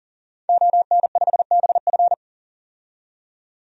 Morse code, heard as a keyed tone, about 39 wpm, 700 Hz. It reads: ON5BF
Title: DXCC-100 Call Signs - 5 Characters - 40wpm